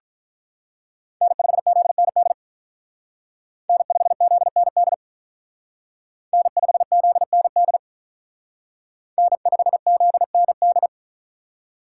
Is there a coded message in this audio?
Yes: Morse code from 1.2 to 10.9 s, heard as a keyed tone, about 45 wpm, 700 Hz. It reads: N5ZND N5ZND N5ZND N5ZND